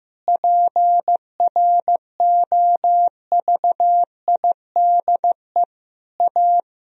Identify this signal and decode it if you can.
Morse code, heard as a keyed tone, about 16 wpm, 700 Hz. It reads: PROVIDE A
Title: Sentences from Top 1000 Words - Review Part 3 - 15wpm